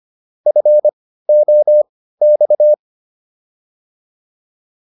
Morse code, heard as a keyed tone, about 25 wpm, spaced as for 15 wpm, 600 Hz. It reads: FOX